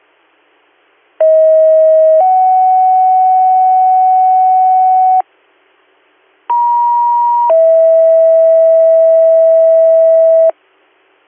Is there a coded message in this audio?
Yes: two-tone page from 1.2 to 10.5 s, heard as a one-second tone, then a three-second tone, about 640 then 750 Hz, about 950 then 650 Hz.